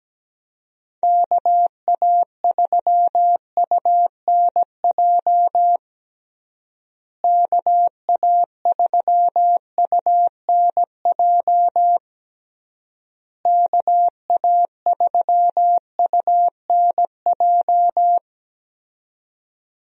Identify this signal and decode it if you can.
Morse code, heard as a keyed tone, about 17 wpm, 700 Hz. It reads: KA3UNJ KA3UNJ KA3UNJ
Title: USA Call Signs - 6 Characters - Repeated 3x - 17wpm